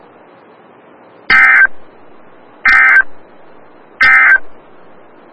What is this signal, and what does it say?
Emergency Alert System (SAME) tones, heard as harsh buzzing bursts. It is heard from 1.2 to 4.5 s.
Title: Des Moines, IA NOAA Weather Radio - WXL57